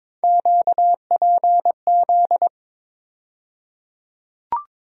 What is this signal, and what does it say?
Morse code, heard as a keyed tone, about 21 wpm, 700 Hz. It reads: QPZ
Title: USA Call Sign Postfixes - 3 Letters 22wpm